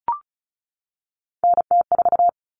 Morse code, heard as a keyed tone, about 34 wpm, 700 Hz. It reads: NT4
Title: Common Cut Numbers - 3-Digit 35wpm